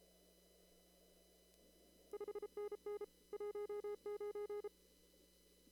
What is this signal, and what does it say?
Morse code, heard as a keyed tone, about 32 wpm, 410 Hz. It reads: 5NN 19
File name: CW / kt5j wpx cw-op-k5pi